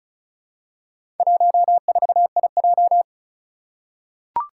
Morse code, heard as a keyed tone, about 36 wpm, 700 Hz. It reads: 14IJ
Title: Four Challenging Letter-Number Combinations 35wpm